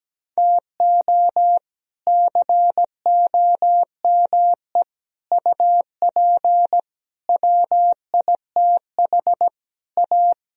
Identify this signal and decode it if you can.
Morse code, heard as a keyed tone, about 17 wpm, 700 Hz. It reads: TO COME UP WITH A
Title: Speed-Racing - Sets of 5 Words 17wpm